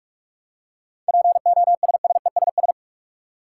Morse code, heard as a keyed tone, about 46 wpm, 700 Hz. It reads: POSSESS
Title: Seven Letter Words - 45wpm